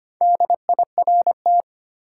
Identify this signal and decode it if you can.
Morse code, heard as a keyed tone, about 25 wpm, 700 Hz. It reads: DIRT
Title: Four Letter Words - 25wpm